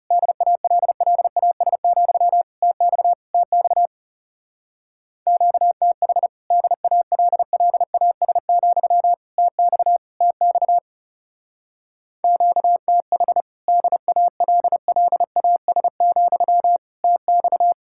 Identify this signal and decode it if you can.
Morse code, heard as a keyed tone, about 36 wpm, 700 Hz. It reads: DALLAS, TX TX QTH DALLAS, TX TX QTH DALLAS, TX
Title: Speed-Racing Common QSO Phrases Encore - 30wpm